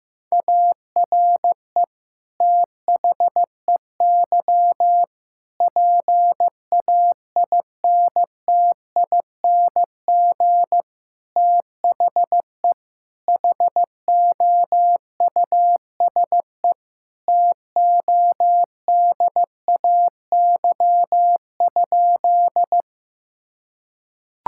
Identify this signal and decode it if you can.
Morse code, heard as a keyed tone, about 15 wpm, 700 Hz. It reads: ARE THEY PAINTING THE HOUSE TODAY?
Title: Sentences from Top 1000 Words - Review Part 6 - 15wpm